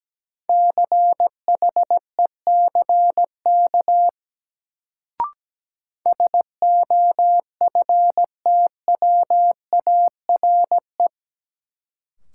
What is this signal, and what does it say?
Morse code, heard as a keyed tone, about 17 wpm, 700 Hz. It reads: CHECK SOFTWARE
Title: Sentences from Top 1000 Words - Review Part 3 - New Word Drill - 17wpm